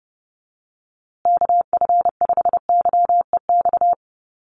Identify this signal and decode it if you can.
Morse code, heard as a keyed tone, about 28 wpm, 700 Hz. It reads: KF5YEX
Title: USA Call Signs Repeated 3x 30wpm